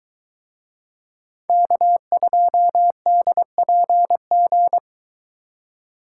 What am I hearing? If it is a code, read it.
Morse code, heard as a keyed tone, about 22 wpm, 700 Hz. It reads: K2DPG